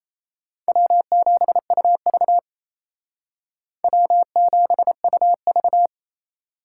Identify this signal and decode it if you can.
Morse code, heard as a keyed tone, about 36 wpm, 700 Hz. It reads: W7UV W7UV